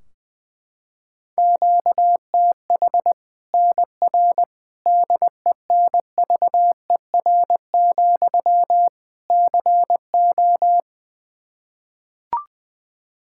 Morse code, heard as a keyed tone, about 20 wpm, 700 Hz. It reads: QTH NR DENVER, CO
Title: K1USN SST QSO Practice 20wpm